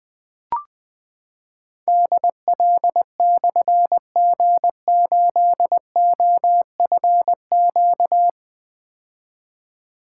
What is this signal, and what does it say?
Morse code, heard as a keyed tone, about 20 wpm, 700 Hz. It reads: DL/G8OFQ
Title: Difficult Call Signs 20wpm